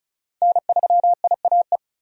Morse code, heard as a keyed tone, about 36 wpm, 700 Hz. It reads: N3IAE
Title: USA Call Signs - 5 Characters - 35wpm